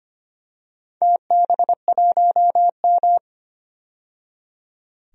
Morse code, heard as a keyed tone, about 25 wpm, 700 Hz. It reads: TB1M
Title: DXCC-100 Call Signs - 4 Characters - 25wpm